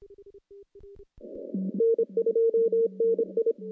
Morse code, heard as a keyed tone, about 26 wpm, 460 Hz. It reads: N2NI
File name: CW-SS-k5tr